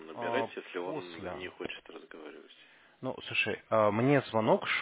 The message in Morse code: none

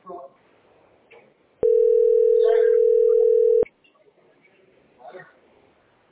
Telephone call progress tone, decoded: ringback tone

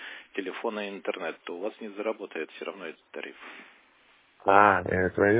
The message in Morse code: none

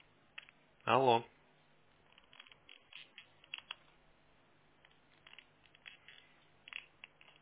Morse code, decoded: none

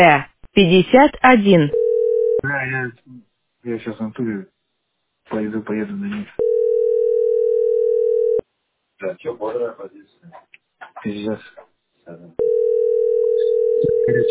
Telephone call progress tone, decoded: ringback tone

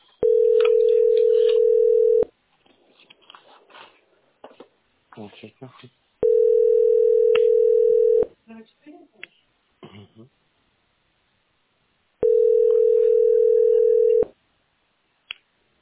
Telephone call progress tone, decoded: ringback tone

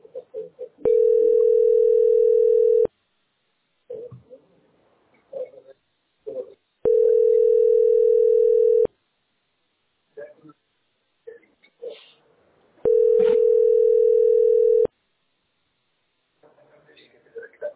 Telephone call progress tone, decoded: ringback tone

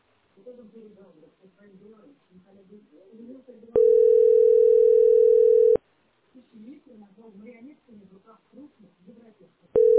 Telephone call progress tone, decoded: ringback tone

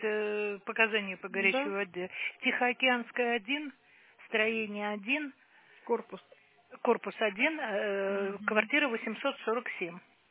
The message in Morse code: none

